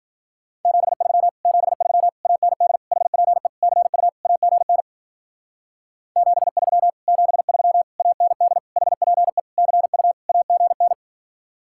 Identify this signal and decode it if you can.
Morse code, heard as a keyed tone, about 47 wpm, 700 Hz. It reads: 73 73 AND HPE CU AGN 73 73 AND HPE CU AGN
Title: Speed-Racing Common QSO Phrases - 40wpm